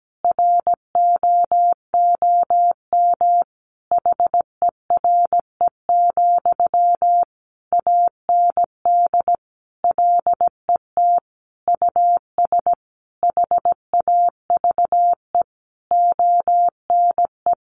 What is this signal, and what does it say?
Morse code, heard as a keyed tone, about 17 wpm, 700 Hz. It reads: ROOM HERE, AND LET US HAVE ONE
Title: Sentences from Top 1000 Words 17wpm